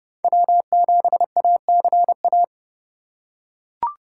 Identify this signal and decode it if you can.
Morse code, heard as a keyed tone, about 30 wpm, 700 Hz. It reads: W7ACA
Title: USA Call Signs - 5 Characters - 3x Character Spacing - 30wpm